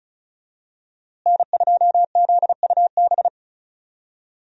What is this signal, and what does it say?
Morse code, heard as a keyed tone, about 34 wpm, 700 Hz. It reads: N2ZUB
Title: USA Call Signs - 5 Characters - Repeated 3x - 35wpm